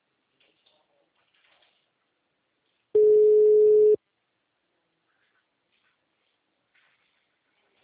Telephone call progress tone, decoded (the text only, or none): ringback tone